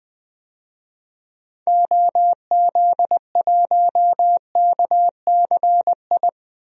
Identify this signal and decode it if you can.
Morse code, heard as a keyed tone, about 20 wpm, 700 Hz. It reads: OZ1KCI
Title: DXCC-100 Call Signs - 6 Characters - 20wpm